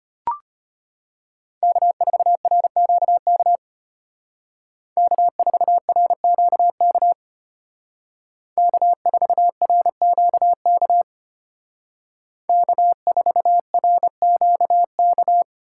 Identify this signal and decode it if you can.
Morse code, heard as a keyed tone, about 33 wpm, 700 Hz. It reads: K4RQK K4RQK K4RQK K4RQK